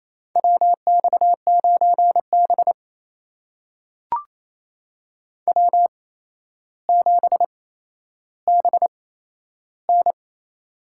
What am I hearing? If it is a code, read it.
Morse code, heard as a keyed tone, about 28 wpm, spaced as for 12 wpm, 700 Hz. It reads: WX9B W7BN
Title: USA Call Signs - 4 Characters - 8x Character Spacing - 28wpm